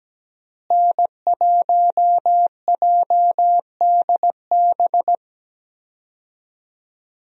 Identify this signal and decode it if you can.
Morse code, heard as a keyed tone, about 17 wpm, 700 Hz. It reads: N1JDB
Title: USA Call Signs - 5 Characters - Repeated 2x - 17wpm